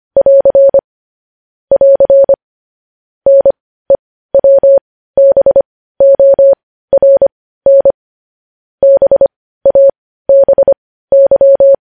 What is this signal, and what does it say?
Morse code, heard as a keyed tone, about 25 wpm, spaced as for 15 wpm, 550 Hz. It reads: <AR> <AR> NEWBORN BABY